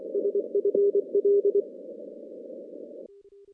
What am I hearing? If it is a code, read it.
Morse code, heard as a keyed tone, about 24 wpm, 400 Hz. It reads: SFL
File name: contests / CW-SS-k5tr- -2200